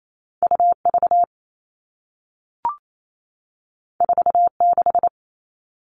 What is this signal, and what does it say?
Morse code, heard as a keyed tone, about 28 wpm, 700 Hz. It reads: UV 46
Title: All Cut Numbers - 2-Digit 28wpm